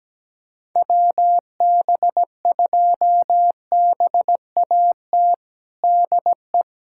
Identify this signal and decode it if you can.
Morse code, heard as a keyed tone, about 17 wpm, 700 Hz. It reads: WB2BAT DE
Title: USA Callsign DE Callsign - 6 Characters - 17wpm